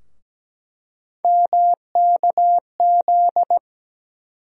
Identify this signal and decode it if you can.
Morse code, heard as a keyed tone, about 17 wpm, 700 Hz. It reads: MKZ
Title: USA Call Sign Postfixes - 3 Letters 17wpm